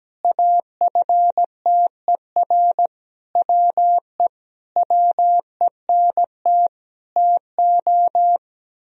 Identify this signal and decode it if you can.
Morse code, heard as a keyed tone, about 17 wpm, 700 Hz. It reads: AFTER WE WENT TO
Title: Sentences from Top 1000 Words - Review Part 2 - 17wpm